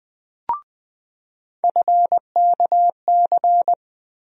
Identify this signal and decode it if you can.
Morse code, heard as a keyed tone, about 20 wpm, 700 Hz. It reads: FKC
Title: USA Call Sign Postfixes - 3 Letters 20wpm